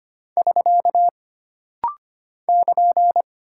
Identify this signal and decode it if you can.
Morse code, heard as a keyed tone, about 25 wpm, 700 Hz. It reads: <SK> <KN>
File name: Single Punctuation-Prosign - Rapid-Fire - 25wpm